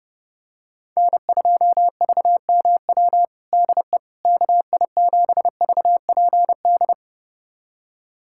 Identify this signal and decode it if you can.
Morse code, heard as a keyed tone, about 32 wpm, 700 Hz. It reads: N2VMW DE KI7VPD
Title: USA Callsign DE Callsign 30wpm